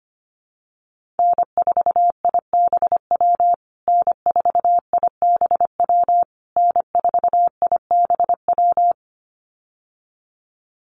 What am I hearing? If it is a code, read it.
Morse code, heard as a keyed tone, about 25 wpm, 700 Hz. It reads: N4IBW N4IBW N4IBW
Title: USA Call Signs Repeated 3x 25wpm